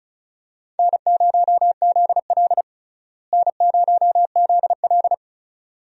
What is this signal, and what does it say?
Morse code, heard as a keyed tone, about 36 wpm, 700 Hz. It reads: N0ZL N0ZL
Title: USA Call Signs - 4 Characters - Repeated 2x - 35wpm